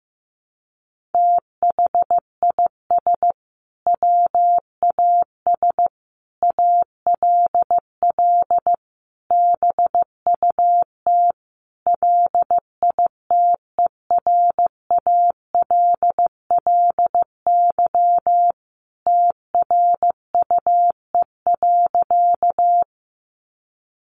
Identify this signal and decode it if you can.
Morse code, heard as a keyed tone, about 15 wpm, 700 Hz. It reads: THIS WAS ALL BUT LITERALLY TRUE.
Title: Sentences from Top 1400 Words 15wpm